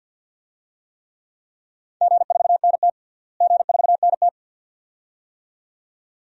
Morse code, heard as a keyed tone, about 52 wpm, 700 Hz. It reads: G4NT G4NT
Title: DXCC-100 Call Signs - 4 Characters - Repeated 2x - 50wpm